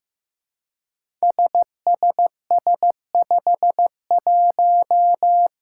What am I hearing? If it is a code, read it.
Morse code, heard as a keyed tone, about 15 wpm, 700 Hz. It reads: SSS51
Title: Five Challenging Letter-Number Combinations 15wpm